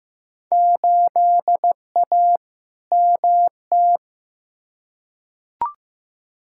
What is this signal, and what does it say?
Morse code, heard as a keyed tone, about 15 wpm, 700 Hz. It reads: 8A MT
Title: Field Day Exchanges Encore - 15wpm